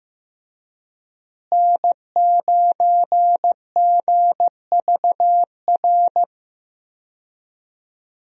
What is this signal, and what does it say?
Morse code, heard as a keyed tone, about 15 wpm, 700 Hz. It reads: N9GVR
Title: USA Call Signs - 5 Characters - Repeated 2x - 15wpm